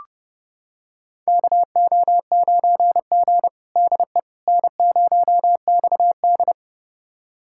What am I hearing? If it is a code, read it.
Morse code, heard as a keyed tone, about 30 wpm, 700 Hz. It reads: KO9G DE N0XD
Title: USA Callsign DE Callsign - 4 Characters - 30wpm